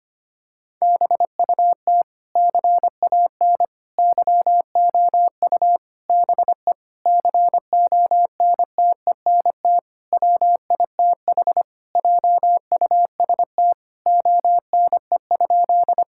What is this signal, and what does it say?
Morse code, heard as a keyed tone, about 25 wpm, 700 Hz. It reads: BUT CAN YOU BE CONTENT WITH JUST ONE?
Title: Sentences from Top 1000 Words - Review Part 6 - 25wpm